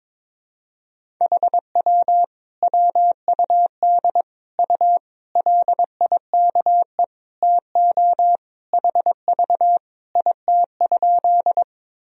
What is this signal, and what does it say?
Morse code, heard as a keyed tone, about 21 wpm, 700 Hz. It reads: HW WUD U LIKE TO HV IT?
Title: Sentences from Top 100 Words and Abbreviations 22wpm